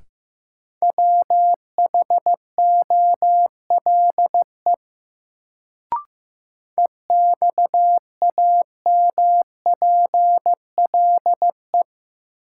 Morse code, heard as a keyed tone, about 15 wpm, 700 Hz. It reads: WHOLE EXAMPLE
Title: Sets of 2 Words - New Word Drill - 15wpm